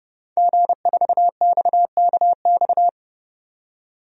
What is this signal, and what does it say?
Morse code, heard as a keyed tone, about 30 wpm, 700 Hz. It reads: G4XKX
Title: DXCC-100 Call Signs - 5 Characters - 30wpm